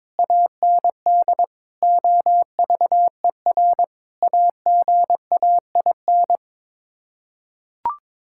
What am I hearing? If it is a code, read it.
Morse code, heard as a keyed tone, about 22 wpm, 700 Hz. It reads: AND OVER AGAIN
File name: practice / 22wpm / Sets of 8 Words - 22wpm